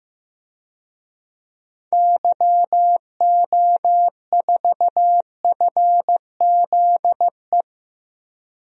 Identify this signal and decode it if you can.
Morse code, heard as a keyed tone, about 15 wpm, 700 Hz. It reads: YO4FZE
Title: DXCC-100 Call Signs - 6 Characters - Repeated 2x - 15wpm